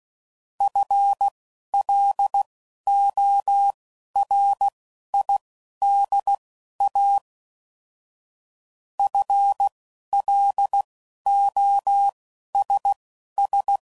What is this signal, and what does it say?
Morse code, heard as a keyed tone, about 16 wpm, spaced as for 11 wpm, 800 Hz. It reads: FLORIDA FLOSS
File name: Prefixes / FLs & THs / FL Runs